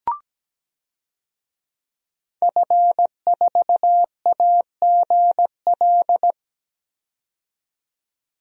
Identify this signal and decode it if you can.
Morse code, heard as a keyed tone, about 17 wpm, 700 Hz. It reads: F4AGL
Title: DXCC-100 Call Signs - 5 Characters - 17wpm